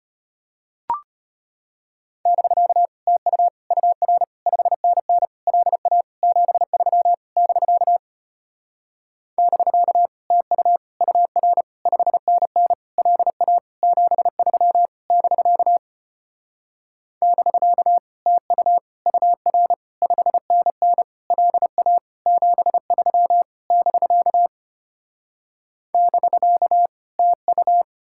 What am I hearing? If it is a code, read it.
Morse code, heard as a keyed tone, about 33 wpm, 700 Hz. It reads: <BK> TU UR 5NN LA 73 <BK> <BK> TU UR 5NN LA 73 <BK> <BK> TU UR 5NN LA 73 <BK> <BK> TU